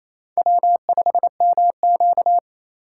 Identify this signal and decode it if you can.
Morse code, heard as a keyed tone, about 36 wpm, 700 Hz. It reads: W5MQ